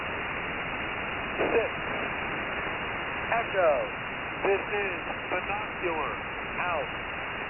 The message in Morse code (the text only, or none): none